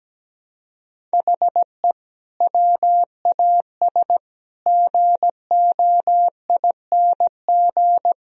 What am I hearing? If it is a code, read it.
Morse code, heard as a keyed tone, about 17 wpm, 700 Hz. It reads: HE WAS GOING
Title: Speed-Racing - Sets of 4 Words 17wpm